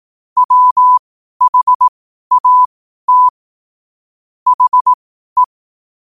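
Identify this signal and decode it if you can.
Morse code, heard as a keyed tone, about 18 wpm, spaced as for 12 wpm, 1000 Hz. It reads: WHAT HE
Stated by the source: Morse code at 12 wpm.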